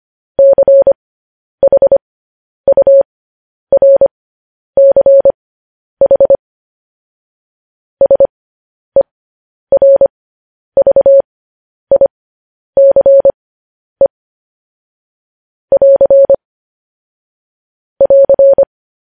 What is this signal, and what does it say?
Morse code, heard as a keyed tone, about 25 wpm, spaced as for 10 wpm, 550 Hz. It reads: CHURCH SERVICE <AR> <AR>
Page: News Headlines in Morse Code at 10 WPM